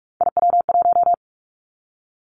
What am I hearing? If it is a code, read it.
Morse code, heard as a keyed tone, about 46 wpm, 700 Hz. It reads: IW1
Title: Callsign Prefixes for Top 25 DXCC - 45wpm